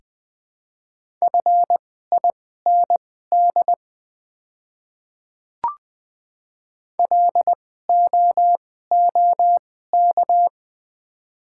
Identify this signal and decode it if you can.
Morse code, heard as a keyed tone, about 20 wpm, spaced as for 15 wpm, 700 Hz. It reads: FIND LOOK
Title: 2x Character Spacing - Top 200 Words - 20wpm